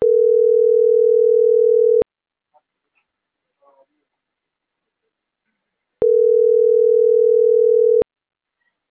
Ringback tone (North America), 440 + 480 Hz.